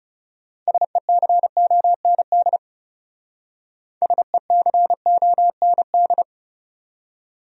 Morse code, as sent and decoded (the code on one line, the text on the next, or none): ... -.-. --- -. -.. / ... -.-. --- -. -..
SECOND SECOND